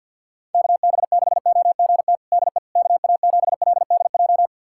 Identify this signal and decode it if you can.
Morse code, heard as a keyed tone, about 51 wpm, 700 Hz. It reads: KB6OGT DE KA7LDJ
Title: USA Callsign DE Callsign - 6 Characters - 50wpm